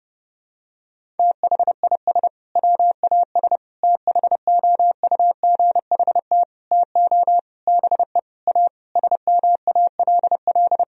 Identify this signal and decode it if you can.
Morse code, heard as a keyed tone, about 30 wpm, 700 Hz. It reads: THIS WAS THOUGHT TO BE A SMALL
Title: Sentences from Top 1000 Words - Review Part 1 - 30wpm